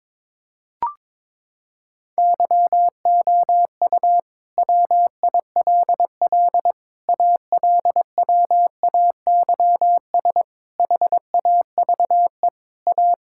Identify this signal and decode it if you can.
Morse code, heard as a keyed tone, about 22 wpm, 700 Hz. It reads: YOU WILL ALWAYS HAVE A
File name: Sentences with 9-Word Phrases - 22wpm